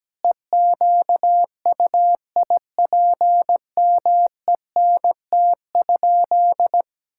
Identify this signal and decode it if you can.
Morse code, heard as a keyed tone, about 17 wpm, 700 Hz. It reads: EQUIPMENT?
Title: Sentences from Top 1000 Words - Review Part 5 - 17wpm